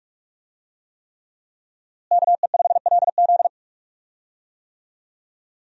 Morse code, heard as a keyed tone, about 45 wpm, 700 Hz. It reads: KE5LZ
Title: USA Call Signs - 5 Characters - Repeated 3x - 45wpm